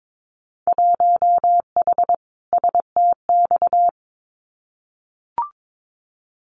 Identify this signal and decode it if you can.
Morse code, heard as a keyed tone, about 22 wpm, 700 Hz. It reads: 1H STX
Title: Winter Field Day Exchanges Encore 22wpm